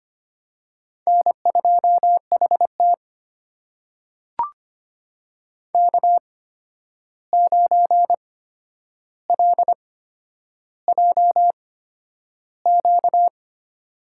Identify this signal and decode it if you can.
Morse code, heard as a keyed tone, about 25 wpm, spaced as for 10 wpm, 700 Hz. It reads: N2HT K9LJQ
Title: USA Call Signs - 8x Character Spacing - 25wpm